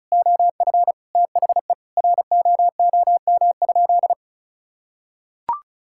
Morse code, heard as a keyed tone, about 36 wpm, 700 Hz. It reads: OF THE ROOM?
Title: Sentences from Top 1000 Words - Review Part 6 - 35wpm